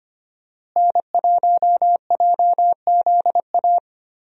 Morse code, heard as a keyed tone, about 25 wpm, 700 Hz. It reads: N1JZA
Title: USA Call Signs - 5 Characters - Repeated 2x - 25wpm